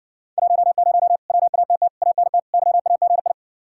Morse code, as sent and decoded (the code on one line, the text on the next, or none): .---- ----- / .-- .- - - / .- -. - / -.-- .- --. ..
10 WATT ANT YAGI